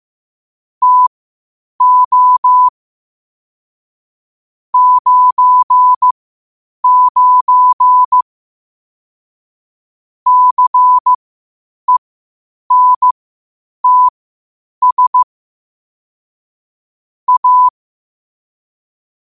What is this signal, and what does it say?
Morse code, heard as a keyed tone, about 15 wpm, spaced as for 8 wpm, 1000 Hz. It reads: TO 99 CENTS A
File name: qotdfiles / 8wpm